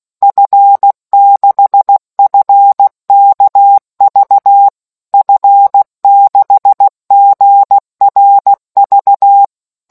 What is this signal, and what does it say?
Morse code, heard as a keyed tone, about 16 wpm, 800 Hz. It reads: F6FKV F6GRV